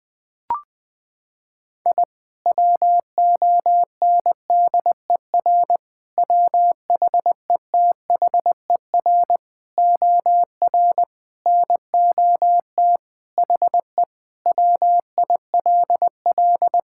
Morse code, heard as a keyed tone, about 20 wpm, 700 Hz. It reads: I WONDER WHETHER OR NOT HE WILL
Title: Sentences with 3-Word Phrases - 20wpm